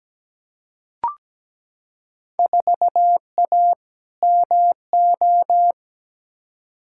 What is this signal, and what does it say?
Morse code, heard as a keyed tone, about 17 wpm, 700 Hz. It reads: 4A MO
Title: Field Day Exchanges - 17wpm